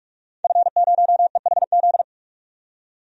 Morse code, heard as a keyed tone, about 44 wpm, 690 Hz. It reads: F0EHZ
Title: DXCC-100 Call Signs - 5 Characters - 45wpm